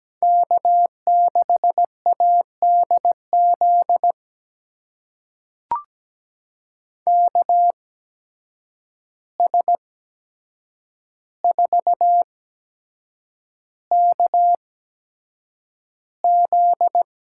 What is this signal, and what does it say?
Morse code, heard as a keyed tone, about 17 wpm, spaced as for 7 wpm, 700 Hz. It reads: K6ADZ KS4KZ
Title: USA Call Signs - 8x Character Spacing - 17wpm